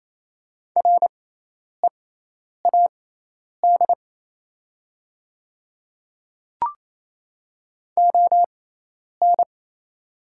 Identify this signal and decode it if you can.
Morse code, heard as a keyed tone, about 28 wpm, spaced as for 10 wpm, 700 Hz. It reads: READ ON
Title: 6x Character Spacing - Top 500 Words - 28wpm